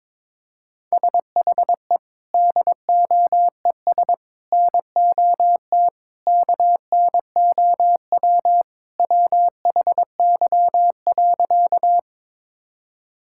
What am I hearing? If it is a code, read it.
Morse code, heard as a keyed tone, about 22 wpm, 700 Hz. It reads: SHE DOES NOT KNOW WHY.
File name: practice / 22wpm / Sentences with 4-Word Phrases - 22wpm